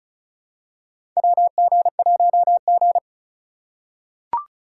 Morse code, heard as a keyed tone, about 36 wpm, 700 Hz. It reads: WG1G
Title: USA Call Signs - 4 Characters - 2x Character Spacing - 35wpm